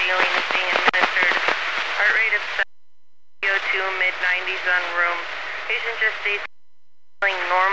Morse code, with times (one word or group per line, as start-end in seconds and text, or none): none